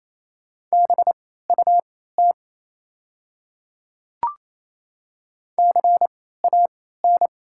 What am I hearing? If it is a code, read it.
Morse code, heard as a keyed tone, about 28 wpm, spaced as for 16 wpm, 690 Hz. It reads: BUT CAN
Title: 3x Character Spacing - Top 100 Words - 28wpm